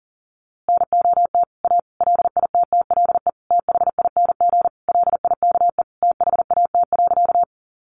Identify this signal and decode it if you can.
Morse code, heard as a keyed tone, about 40 wpm, 700 Hz. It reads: NOT A LITTLE THING LIKE THAT.
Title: A Sound Of Thunder 40wpm